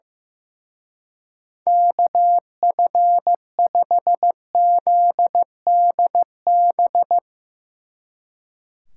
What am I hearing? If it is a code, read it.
Morse code, heard as a keyed tone, about 16 wpm, 700 Hz. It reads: KF5ZDB
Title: Speed Racing - USA 2x3 Call Signs 15wpm